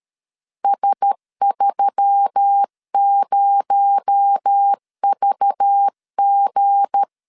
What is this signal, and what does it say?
Morse code, heard as a keyed tone, about 13 wpm, 800 Hz. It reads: S30VG
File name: LEZIONI / 8 YJ45 / 8 Y 12wpm 2228